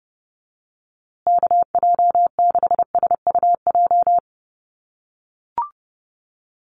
Morse code, heard as a keyed tone, about 30 wpm, 700 Hz. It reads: KJ6SUJ
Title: USA Call Signs Repeated 2x 30wpm